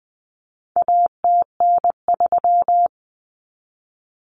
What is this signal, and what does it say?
Morse code, heard as a keyed tone, about 20 wpm, 700 Hz. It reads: ATN3
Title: Common Cut Numbers - 4-Digit 20wpm